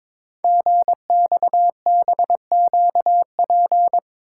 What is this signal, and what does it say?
Morse code, heard as a keyed tone, about 22 wpm, 700 Hz. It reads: GXBQP
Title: Code Groups - 5 Letters - 22wpm